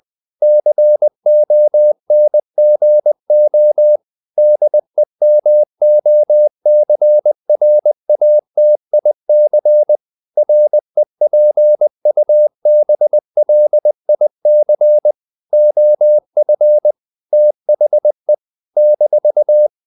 Morse code, heard as a keyed tone, about 20 wpm, 600 Hz. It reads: CONGO DEMOCRATIC REPUBLIC OF THE -
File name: CW / Practice / NationalCapitals20